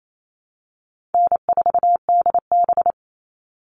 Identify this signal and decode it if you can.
Morse code, heard as a keyed tone, about 28 wpm, 700 Hz. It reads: N4DB
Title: All Cut Numbers - 4-Digit 28wpm